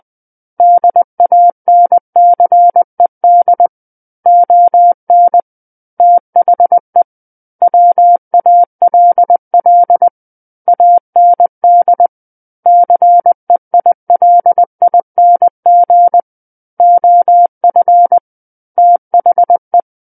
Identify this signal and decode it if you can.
Morse code, heard as a keyed tone, about 20 wpm, 700 Hz. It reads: DANCED ON THE WALL AND CEILING OF THE
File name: 20 WPM